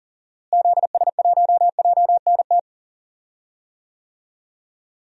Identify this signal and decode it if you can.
Morse code, heard as a keyed tone, about 37 wpm, 700 Hz. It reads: ZS1JNT
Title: DXCC-100 Call Signs - 6 Characters - Repeated 3x - 40wpm